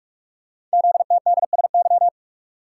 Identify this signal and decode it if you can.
Morse code, heard as a keyed tone, about 44 wpm, 700 Hz. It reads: ZTDSY
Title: Code Groups - 5 Letters - 45wpm